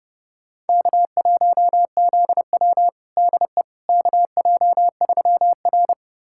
Morse code, heard as a keyed tone, about 30 wpm, 700 Hz. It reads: K1ZW DE KJ3R
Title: USA Callsign DE Callsign - 4 Characters - 30wpm